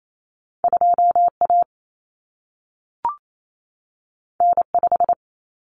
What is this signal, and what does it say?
Morse code, heard as a keyed tone, about 28 wpm, 700 Hz. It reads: 2A N5
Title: Common Cut Numbers - 2-Digit 28wpm